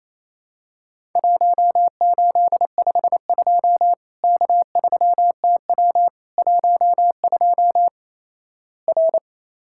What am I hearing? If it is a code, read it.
Morse code, heard as a keyed tone, about 28 wpm, 700 Hz. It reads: 1852 K3TW 12 R
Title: Worked All Europe DX Contest - QTC Exchange - 28wpm